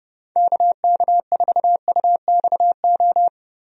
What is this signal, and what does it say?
Morse code, heard as a keyed tone, about 30 wpm, 700 Hz. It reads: KK4UXO
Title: USA Call Signs - 6 Characters - Repeated 3x - 30wpm